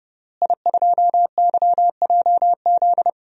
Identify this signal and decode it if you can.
Morse code, heard as a keyed tone, about 29 wpm, 700 Hz. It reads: I2YJZ
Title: DXCC-100 Call Signs - 5 Characters - Repeated 3x - 30wpm